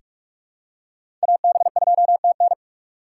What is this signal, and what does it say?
Morse code, heard as a keyed tone, about 47 wpm, 700 Hz. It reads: AB2TN
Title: USA Call Signs - 6x Character Spacing - 45wpm